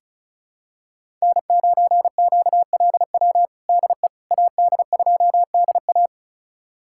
Morse code, heard as a keyed tone, about 36 wpm, 700 Hz. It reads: N9QLW DE AD2DA
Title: USA Callsign DE Callsign - 5 Characters - 35wpm